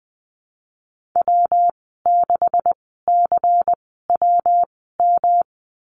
Morse code, heard as a keyed tone, about 20 wpm, spaced as for 14 wpm, 700 Hz. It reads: W6CWM
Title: 128 Lesson 21g - Number 6 - Call Signs 20wpm